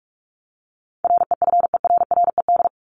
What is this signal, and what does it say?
Morse code, heard as a keyed tone, about 44 wpm, 700 Hz. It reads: REFERRED
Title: ED Suffix - Top 100 Words with Base Words 45wpm